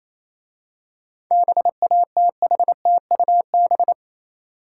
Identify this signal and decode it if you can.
Morse code, heard as a keyed tone, about 28 wpm, 700 Hz. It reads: BATHTUB